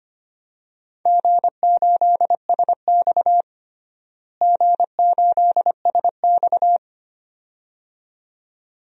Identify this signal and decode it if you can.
Morse code, heard as a keyed tone, about 25 wpm, 700 Hz. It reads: G8SX G8SX
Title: DXCC-100 Call Signs - 4 Characters - Repeated 2x - 25wpm